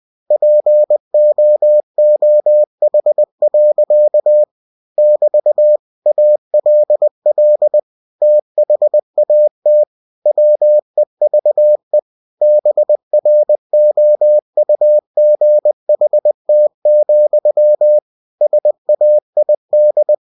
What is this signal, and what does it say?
Morse code, heard as a keyed tone, about 20 wpm, 600 Hz. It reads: POOH. <BT> ALL THAT WEVE BROUGHT, SAID
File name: cw-books / winnie-the-pooh / winnie-20wpm-page-0083